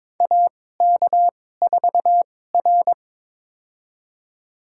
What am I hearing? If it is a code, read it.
Morse code, heard as a keyed tone, about 22 wpm, spaced as for 16 wpm, 700 Hz. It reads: AK4R
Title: USA Call Signs - 4 Characters - 2x Character Spacing - 22wpm